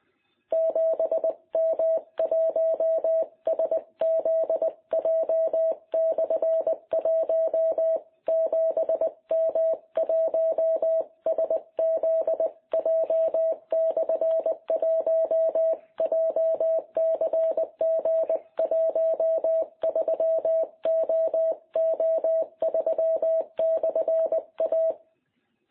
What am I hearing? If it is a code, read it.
Morse code, heard as a keyed tone, about 20 wpm, 660 Hz. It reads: 7M1SZJ/1 7M1SZJ/1 JCG13OO3/A
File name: jh8cbh / 2021cw